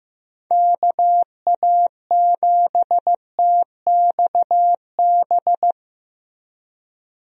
Morse code, heard as a keyed tone, about 15 wpm, 700 Hz. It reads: KA7TXB
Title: USA Call Signs - 6 Characters - Repeated 2x - 15wpm